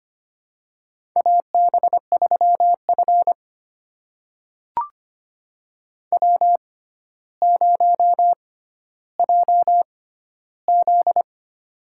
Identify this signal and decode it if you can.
Morse code, heard as a keyed tone, about 25 wpm, spaced as for 13 wpm, 700 Hz. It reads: AB3F W0JZ
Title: USA Call Signs - 4 Characters - 6x Character Spacing - 25wpm